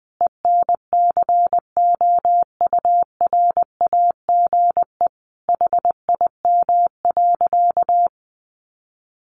Morse code, heard as a keyed tone, about 20 wpm, 700 Hz. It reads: ENCOURAGE HIM.
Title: Sentences from Top 1400 Words 20wpm